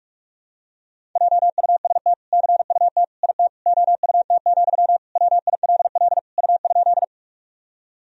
Morse code, heard as a keyed tone, about 41 wpm, 700 Hz. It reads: JUST CUT IT OUT, WILL U?
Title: Sentences from Top 300 Words and Abbreviations 45wpm